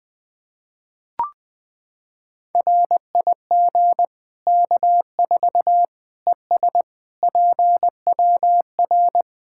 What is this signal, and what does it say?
Morse code, heard as a keyed tone, about 21 wpm, 700 Hz. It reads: RIG K4 ES PWR
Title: K1USN SST QSO Practice 20wpm